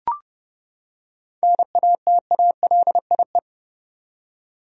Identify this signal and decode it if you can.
Morse code, heard as a keyed tone, about 31 wpm, 700 Hz. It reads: NATALIE
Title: USA Common Names - 30wpm